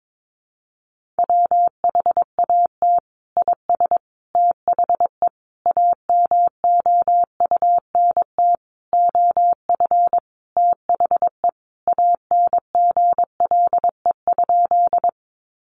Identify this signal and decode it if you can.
Morse code, heard as a keyed tone, about 22 wpm, 700 Hz. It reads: WHAT IS THE AMOUNT OF THE ANGLE?
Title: Sentences from Top 1500 Words 22wpm